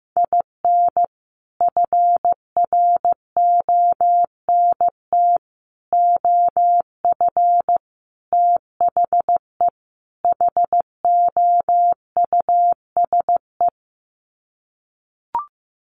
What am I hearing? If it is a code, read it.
Morse code, heard as a keyed tone, about 15 wpm, 700 Hz. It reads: IN FRONT OF THE HOUSE
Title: Sets of 5 Words Encore2 15wpm